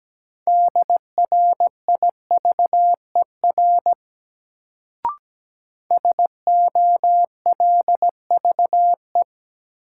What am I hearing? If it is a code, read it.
Morse code, heard as a keyed tone, about 17 wpm, 700 Hz. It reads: DRIVER SOLVE
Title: Sentences from Top 1000 Words - New Word Drill - 17wpm